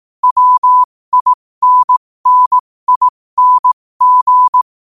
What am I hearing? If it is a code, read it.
Morse code, heard as a keyed tone, about 18 wpm, spaced as for 15 wpm, 1000 Hz. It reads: WINNING
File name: qotdfiles / 15wpm